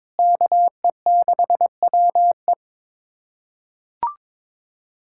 Morse code, heard as a keyed tone, about 22 wpm, 700 Hz. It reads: KE6WE
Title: USA Callsign DE Callsign - 5 Characters - 22wpm